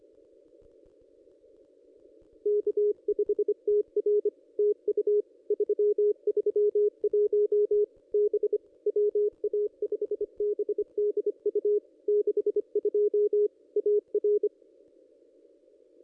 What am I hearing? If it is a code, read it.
Morse code, heard as a keyed tone, about 25 wpm, 420 Hz. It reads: K5TR TU 331 B WA5BDU 62 AR
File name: contests / ss-cw-k5tr-op-n5rz -1600